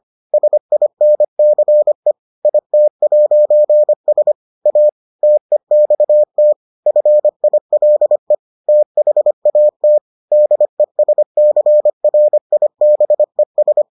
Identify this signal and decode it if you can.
Morse code, heard as a keyed tone, about 25 wpm, 600 Hz. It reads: SINCE IT'S A TEXT FILE THAT DESCRIBES